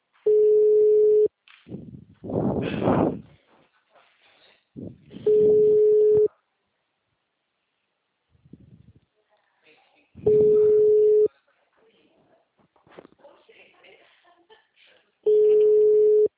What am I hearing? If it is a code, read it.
Ringback tone (Europe, 425 Hz), with speech.